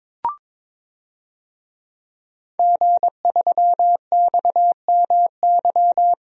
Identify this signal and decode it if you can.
Morse code, heard as a keyed tone, about 21 wpm, 700 Hz. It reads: G3XMY
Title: DXCC-100 Call Signs - 5 Characters - Repeated 2x - 22wpm